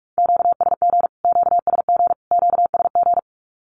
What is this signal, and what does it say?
Morse code, heard as a keyed tone, about 44 wpm, 700 Hz. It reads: QSG QSG QSG
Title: Q-Codes Repeated 3x 45wpm